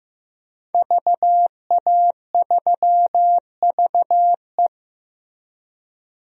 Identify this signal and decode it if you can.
Morse code, heard as a keyed tone, about 15 wpm, 700 Hz. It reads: VA3VE
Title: DXCC-100 Call Signs - 5 Characters - Repeated 3x - 15wpm